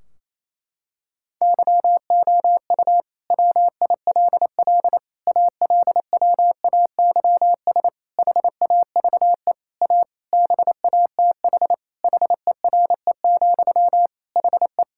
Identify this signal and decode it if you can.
Morse code, heard as a keyed tone, about 28 wpm, 700 Hz. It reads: YOU WILL ALWAYS HAVE A BATH HERE, HE
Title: Sentences from Top 1600 Words 28wpm